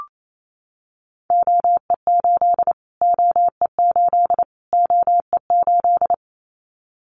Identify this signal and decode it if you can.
Morse code, heard as a keyed tone, about 28 wpm, 700 Hz. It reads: OE8 OE8 OE8
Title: Callsign Prefixes for Top 25 DXCC - Repeated 3x - 28wpm